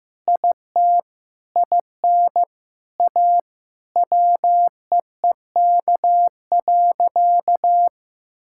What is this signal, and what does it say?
Morse code, heard as a keyed tone, about 15 wpm, 700 Hz. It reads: IT IN A WEEK.
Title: Sentences from Top 1000 Words - Review Part 5 - 15wpm